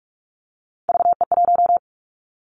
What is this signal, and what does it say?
Morse code, heard as a keyed tone, about 46 wpm, 700 Hz. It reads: VE1
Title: Callsign Prefixes for Top 100 DXCC - 45wpm